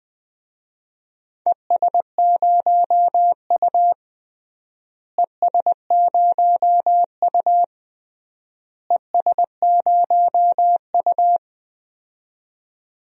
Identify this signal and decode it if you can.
Morse code, heard as a keyed tone, about 20 wpm, 700 Hz. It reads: ES0U ES0U ES0U
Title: DXCC-100 Call Signs - 4 Characters - Repeated 3x - 20wpm